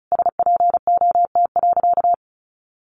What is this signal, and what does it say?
Morse code, heard as a keyed tone, about 34 wpm, 700 Hz. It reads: SPOT.
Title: Sentences from Top 1100 Words 35wpm